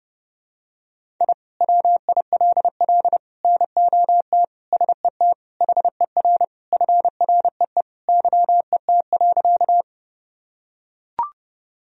Morse code, heard as a keyed tone, about 30 wpm, 700 Hz. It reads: I WILL NOT SET HER FREE YET.
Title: Sentences from Top 1000 Words - Review Part 1 - 30wpm